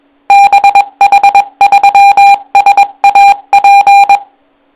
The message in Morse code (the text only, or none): BH3SAP